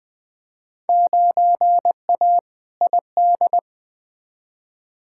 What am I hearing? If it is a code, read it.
Morse code, heard as a keyed tone, about 20 wpm, 700 Hz. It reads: 9A ID
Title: Field Day Exchanges Encore - 20wpm